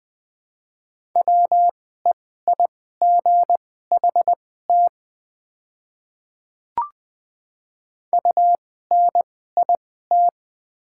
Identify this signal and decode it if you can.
Morse code, heard as a keyed tone, about 20 wpm, spaced as for 14 wpm, 700 Hz. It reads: WEIGHT UNIT
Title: 2x Character Spacing - Top 500 Words - 20wpm